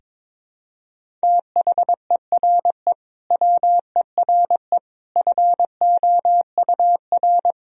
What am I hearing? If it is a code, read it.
Morse code, heard as a keyed tone, about 21 wpm, 700 Hz. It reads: THERE WERE FOUR
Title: Sentences from Top 1000 Words - Review Part 1 - 22wpm